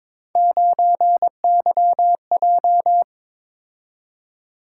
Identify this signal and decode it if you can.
Morse code, heard as a keyed tone, about 22 wpm, 700 Hz. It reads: 9YJ
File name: practice / 22wpm / Three Letter-Number Combinations - 22wpm